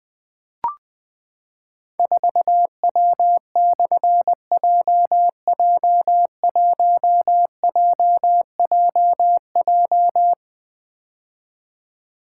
Morse code, heard as a keyed tone, about 20 wpm, 700 Hz. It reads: 4W/JJ1JJJ
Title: Difficult Call Signs 20wpm